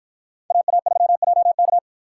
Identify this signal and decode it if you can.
Morse code, heard as a keyed tone, about 53 wpm, 700 Hz. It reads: AA3JK